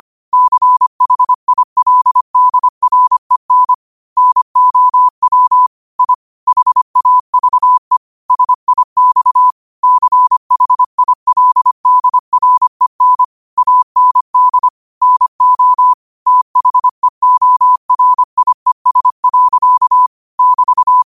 Morse code, heard as a keyed tone, about 25 wpm, 1000 Hz. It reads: CHILDREN NOW I HAVE SIX CHILDREN AND NO THEORIES. <BT>
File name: qotdfiles / 25wpm